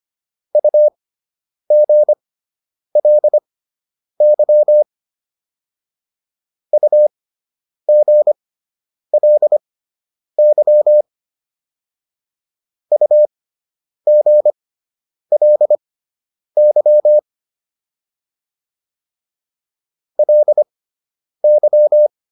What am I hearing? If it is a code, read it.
Morse code, heard as a keyed tone, about 25 wpm, spaced as for 9 wpm, 600 Hz. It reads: UGLY UGLY UGLY LY